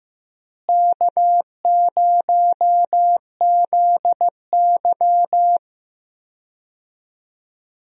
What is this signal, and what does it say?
Morse code, heard as a keyed tone, about 15 wpm, 700 Hz. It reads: K0ZY
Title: USA Call Signs - 4 Characters - Repeated 2x - 15wpm